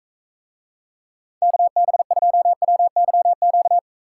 Morse code, heard as a keyed tone, about 44 wpm, 700 Hz. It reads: KB2WYQ